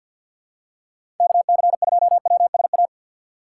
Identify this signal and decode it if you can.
Morse code, heard as a keyed tone, about 48 wpm, 700 Hz. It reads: KC2WSA
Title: USA Call Signs - 6 Characters - 3x Character Spacing - 50wpm